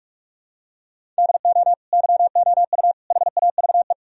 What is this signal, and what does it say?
Morse code, heard as a keyed tone, about 46 wpm, 700 Hz. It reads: DO YOU HAVE
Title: Sentences from Top 1000 Words - Review Part 3 - 45wpm